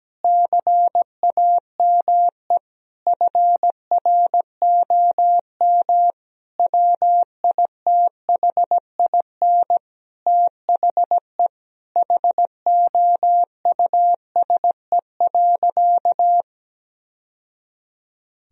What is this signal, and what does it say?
Morse code, heard as a keyed tone, about 17 wpm, 700 Hz. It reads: CAME FROM WITHIN THE HOUSE.
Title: Sentences from Top 1000 Words - Review Part 4 - 17wpm